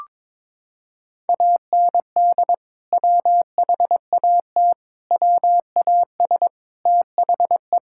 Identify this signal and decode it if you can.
Morse code, heard as a keyed tone, about 22 wpm, 700 Hz. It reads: AND WHAT WAS THE
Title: Sentences from Top 1000 Words - Review Part 5 - 22wpm